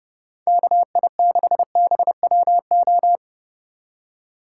Morse code, heard as a keyed tone, about 34 wpm, 700 Hz. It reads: KI6BWO